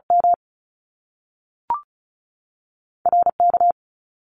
Morse code, heard as a keyed tone, about 36 wpm, spaced as for 6 wpm, 700 Hz. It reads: MRK
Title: Top Two Letter Combinations in Words 35wpm